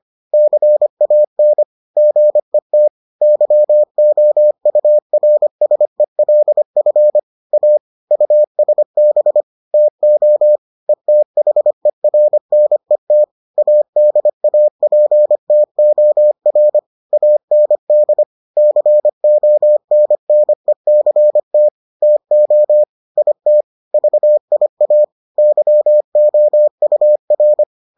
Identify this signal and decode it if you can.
Morse code, heard as a keyed tone, about 25 wpm, 600 Hz. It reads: CAN GET YOURSELF A USB TO ETHERNET ADAPTOR AND CONNECT TO IT VIA YOUR